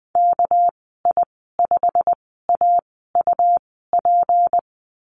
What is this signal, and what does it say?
Morse code, heard as a keyed tone, about 20 wpm, spaced as for 15 wpm, 700 Hz. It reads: KI5AUP
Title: 238 Lesson 38g - Letter q - Call Signs 20wpm